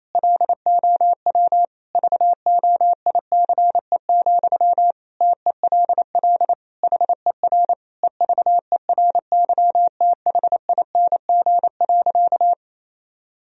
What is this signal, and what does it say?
Morse code, heard as a keyed tone, about 28 wpm, 700 Hz. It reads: LOW VOICE, TELL HER EVERYTHING.
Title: Sentences from Top 1000 Words - Review Part 1 - 28wpm